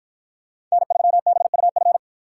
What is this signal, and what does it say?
Morse code, heard as a keyed tone, about 52 wpm, 700 Hz. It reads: N3BUF